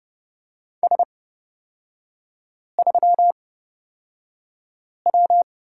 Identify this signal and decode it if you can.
Morse code, heard as a keyed tone, about 30 wpm, spaced as for 5 wpm, 700 Hz. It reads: S3W
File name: Lesson 36e - BT - Three Characters 30wpm